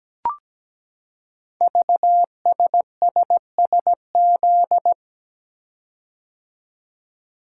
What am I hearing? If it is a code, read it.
Morse code, heard as a keyed tone, about 17 wpm, 700 Hz. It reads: VSSSZ
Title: Code Groups - 5 Letters - 17wpm